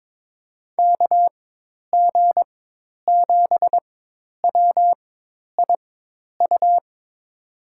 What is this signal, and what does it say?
Morse code, heard as a keyed tone, about 22 wpm, spaced as for 10 wpm, 700 Hz. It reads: KG7WIU
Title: USA Call Signs - 6 Characters - 4x Character Spacing - 22wpm